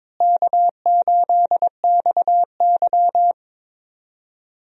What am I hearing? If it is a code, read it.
Morse code, heard as a keyed tone, about 22 wpm, 700 Hz. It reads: K8XY
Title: USA Call Signs - 4 Characters - Repeated 3x - 22wpm